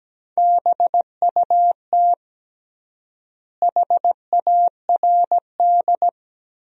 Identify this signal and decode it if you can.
Morse code, heard as a keyed tone, about 17 wpm, 700 Hz. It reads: BUT HARD
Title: 3x Word-Spacing - Sentences from Top 300 Words - 17wpm